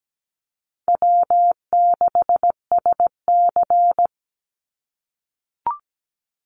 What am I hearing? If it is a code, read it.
Morse code, heard as a keyed tone, about 17 wpm, 700 Hz. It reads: W6SC
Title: Easy Call Signs - 17wpm